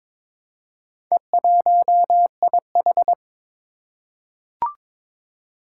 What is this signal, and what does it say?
Morse code, heard as a keyed tone, about 22 wpm, 700 Hz. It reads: E1IH
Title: Four Challenging Letter-Number Combinations 22wpm